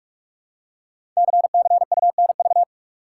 Morse code, heard as a keyed tone, about 44 wpm, 690 Hz. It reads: CCUNV
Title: Code Groups - 5 Letters - 45wpm